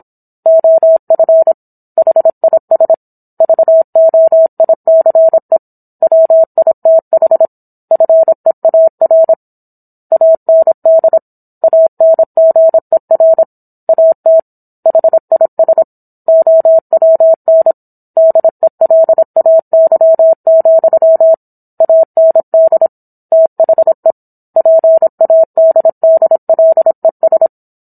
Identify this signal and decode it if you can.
Morse code, heard as a keyed tone, about 27 wpm, 650 Hz. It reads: OF HIS VOICE WITH FEAR AND ANGER AT HIS OWN DELAY, AND THE PADDLES